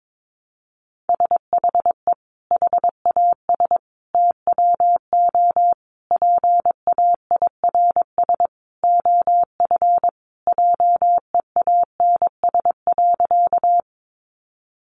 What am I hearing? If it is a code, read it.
Morse code, heard as a keyed tone, about 22 wpm, 700 Hz. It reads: SHE HAS TWO PAIRS OF JEANS.
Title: Additional Easy Phrases 22wpm